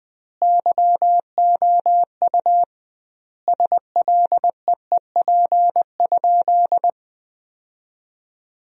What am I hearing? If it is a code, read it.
Morse code, heard as a keyed tone, about 20 wpm, 700 Hz. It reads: YOU SLEEP?
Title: 2x Word-Spacing - Sentences from Top 500 Words - 20wpm